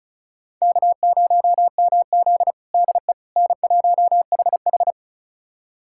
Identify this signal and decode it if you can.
Morse code, heard as a keyed tone, about 34 wpm, 700 Hz. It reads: K0MZ DE N1HH
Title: USA Callsign DE Callsign - 4 Characters - 35wpm